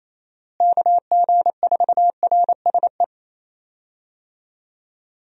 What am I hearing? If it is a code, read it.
Morse code, heard as a keyed tone, about 28 wpm, 700 Hz. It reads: KG4RSE
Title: USA Call Signs - 6 Characters - Repeated 2x - 28wpm